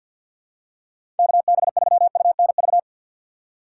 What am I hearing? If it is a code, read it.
Morse code, heard as a keyed tone, about 53 wpm, 700 Hz. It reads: KB3UNV
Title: USA Call Signs - 6 Characters - 6x Character Spacing - 50wpm